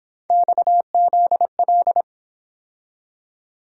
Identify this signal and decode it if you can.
Morse code, heard as a keyed tone, about 26 wpm, 700 Hz. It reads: XZL
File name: Speed-Racing - Three Letter-Number Combinations - 22wpm